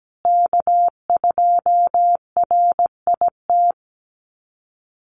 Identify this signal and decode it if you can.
Morse code, heard as a keyed tone, about 17 wpm, 700 Hz. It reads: K2RIT
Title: USA Call Signs Repeated 2x 17wpm